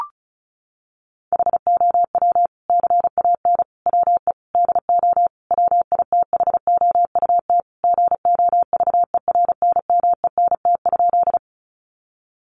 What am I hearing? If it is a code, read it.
Morse code, heard as a keyed tone, about 34 wpm, 700 Hz. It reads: HOW CAN WE DO WITHOUT GOVERNMENT?
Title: Sentences from Top 600 Words 35wpm